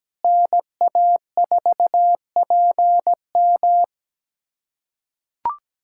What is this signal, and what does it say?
Morse code, heard as a keyed tone, about 17 wpm, 700 Hz. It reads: NA4PM
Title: USA Callsign DE Callsign - 5 Characters - 17wpm